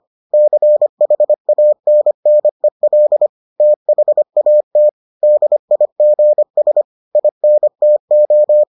Morse code, heard as a keyed tone, about 25 wpm, 600 Hz. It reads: CHANNEL THAT DIGS INTO